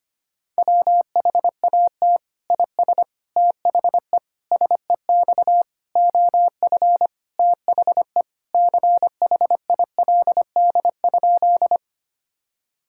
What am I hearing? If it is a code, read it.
Morse code, heard as a keyed tone, about 25 wpm, 700 Hz. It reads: WHAT IS THE SEX OF THE CHILD?
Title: Sentences from Top 1600 Words 25wpm